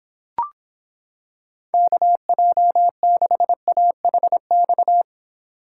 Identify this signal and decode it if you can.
Morse code, heard as a keyed tone, about 26 wpm, 700 Hz. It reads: KJ6AHX